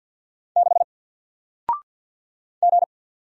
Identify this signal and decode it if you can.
Morse code, heard as a keyed tone, about 50 wpm, 700 Hz. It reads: 6 G
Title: Single Letter-Number 50wpm